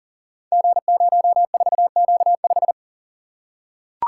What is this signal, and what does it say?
Morse code, heard as a keyed tone, about 42 wpm, 700 Hz. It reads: G04Q5
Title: Code Groups - 5 Letter:Numbers - 40wpm